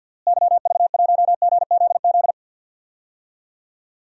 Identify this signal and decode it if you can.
Morse code, heard as a keyed tone, about 51 wpm, 700 Hz. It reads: YV1GZZ
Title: DXCC-100 Call Signs - 6 Characters - Repeated 2x - 50wpm